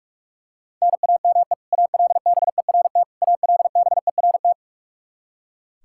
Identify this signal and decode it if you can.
Morse code, heard as a keyed tone, about 46 wpm, 700 Hz. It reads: NAME ALBERT ALBERT
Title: Practice QSO - Exchange 1 - Beginning - 45wpm